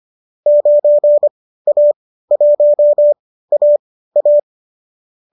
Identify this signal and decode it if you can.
Morse code, heard as a keyed tone, about 25 wpm, spaced as for 15 wpm, 590 Hz. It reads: 9A1AA
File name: qso219 15